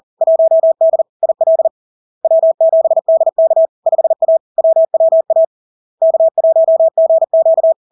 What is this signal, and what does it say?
Morse code, heard as a keyed tone, about 40 wpm, 650 Hz. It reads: 1D IL W7DK 5A WWA K1GQ
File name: LARK-FD-2022-Log-in-CW-40wpm-650-Hz